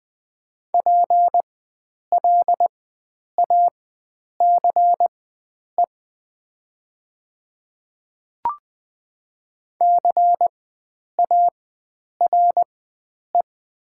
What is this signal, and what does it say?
Morse code, heard as a keyed tone, about 20 wpm, spaced as for 9 wpm, 700 Hz. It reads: PLACE CARE
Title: 4x Character Spacing - Top 500 Words - 20wpm